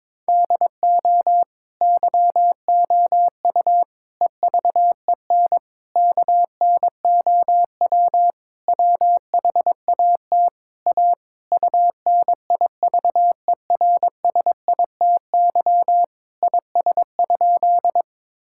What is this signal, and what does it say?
Morse code, heard as a keyed tone, about 22 wpm, 700 Hz. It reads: DO YOU EVEN KNOW WHAT A UNIVERSITY IS?
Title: Sentences from Top 1000 Words - Review Part 5 - 22wpm